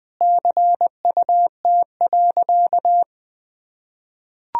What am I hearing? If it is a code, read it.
Morse code, heard as a keyed tone, about 20 wpm, 700 Hz. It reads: CUT.
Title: Sentences from Top 1000 Words - Review Part 1 - 20wpm